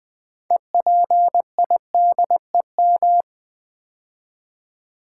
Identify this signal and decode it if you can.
Morse code, heard as a keyed tone, about 20 wpm, 700 Hz. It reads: EPIDEM
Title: Build A Word - 3 Letters At A Time - 8-Letter Words - 20wpm